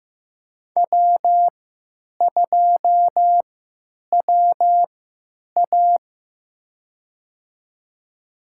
Morse code, heard as a keyed tone, about 15 wpm, spaced as for 8 wpm, 700 Hz. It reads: W2WA
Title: USA Call Signs - 4 Characters - 3x Character Spacing - 15wpm